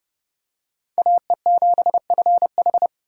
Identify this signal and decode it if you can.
Morse code, heard as a keyed tone, about 29 wpm, 700 Hz. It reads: AE7FH